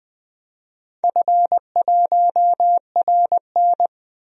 Morse code, heard as a keyed tone, about 20 wpm, 700 Hz. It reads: F1RN
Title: DXCC-100 Call Signs - 4 Characters - Repeated 2x - 20wpm